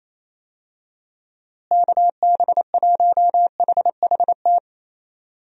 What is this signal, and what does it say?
Morse code, heard as a keyed tone, about 28 wpm, 700 Hz. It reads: KB1HHT
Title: USA Call Signs - 6 Characters - Repeated 3x - 28wpm